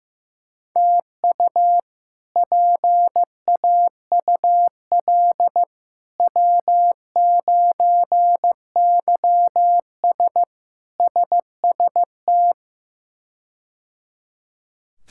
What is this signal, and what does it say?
Morse code, heard as a keyed tone, about 15 wpm, 700 Hz. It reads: TU PAUL W9YS SST
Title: K1USN SST QSO Practice - 15wpm